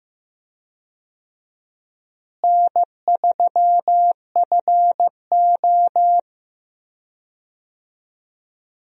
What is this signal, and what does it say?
Morse code, heard as a keyed tone, about 15 wpm, 700 Hz. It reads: N3FO
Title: USA Call Signs - 4 Characters - Repeated 2x - 15wpm